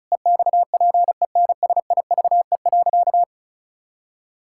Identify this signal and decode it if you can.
Morse code, heard as a keyed tone, about 34 wpm, 700 Hz. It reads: EXPENSIVE.
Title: Pangram Challenge v2 - 35wpm